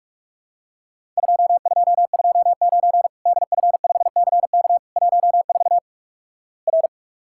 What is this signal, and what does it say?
Morse code, heard as a keyed tone, about 46 wpm, 700 Hz. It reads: 2229 DF5CK 14 R
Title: Worked All Europe DX Contest - QTC Exchange - 45wpm